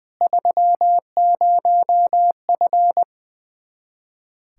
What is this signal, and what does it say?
Morse code, heard as a keyed tone, about 20 wpm, 700 Hz. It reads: 30F
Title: Practice QSO - Exchange 2 - Info - 20wpm